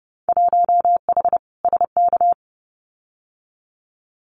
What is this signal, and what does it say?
Morse code, heard as a keyed tone, about 30 wpm, 700 Hz. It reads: 1H SK
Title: Winter Field Day Exchanges 30wpm